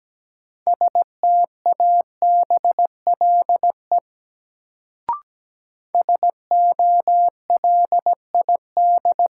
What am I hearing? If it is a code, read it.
Morse code, heard as a keyed tone, about 17 wpm, 700 Hz. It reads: STABLE SOLID
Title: Sentences from Top 1500 Words - New Word Drill - 17wpm